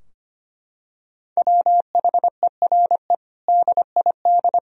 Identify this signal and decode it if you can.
Morse code, heard as a keyed tone, about 25 wpm, 700 Hz. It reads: WHERE DID
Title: Sentences from Top 1000 Words - Review Part 3 - 25wpm